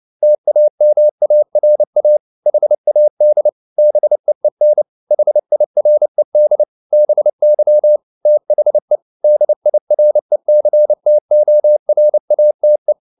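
Morse code, heard as a keyed tone, about 29 wpm, 600 Hz. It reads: TAMARA HAD BEEN HIRED BY THE DIRECTORATE
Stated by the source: Morse code at 29 wpm, sending Ken Follett's novel Never in English, from CW med Gnister.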